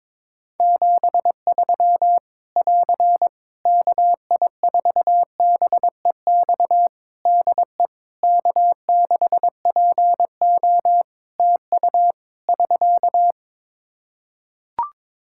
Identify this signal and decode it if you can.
Morse code, heard as a keyed tone, about 21 wpm, 700 Hz. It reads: 73 <AR> KI4BEX DE K6PO TU <SK>
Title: Practice QSO - Exchange 4 - Ending - 22wpm